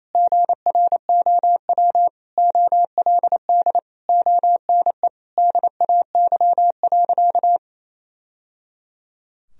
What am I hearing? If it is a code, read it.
Morse code, heard as a keyed tone, about 28 wpm, 700 Hz. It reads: GROW OLD ONE DAY.
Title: Sentences from Top 1000 Words - Review Part 1 - 28wpm